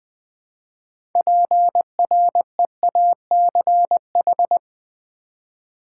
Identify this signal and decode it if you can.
Morse code, heard as a keyed tone, about 20 wpm, 700 Hz. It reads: PREACH
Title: Build A Word - 2 Letters At A Time - 6-Letter Words - 20wpm